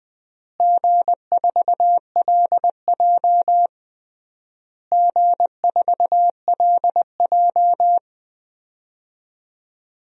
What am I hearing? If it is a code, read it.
Morse code, heard as a keyed tone, about 21 wpm, 700 Hz. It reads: G4LJ G4LJ
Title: DXCC-100 Call Signs - 4 Characters - Repeated 3x - 20wpm